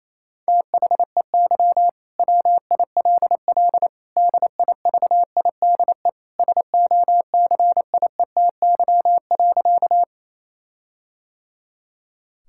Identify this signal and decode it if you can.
Morse code, heard as a keyed tone, about 28 wpm, 700 Hz. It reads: THEY WILL DIVIDE SOCIETY.
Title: Sentences from Top 1600 Words 28wpm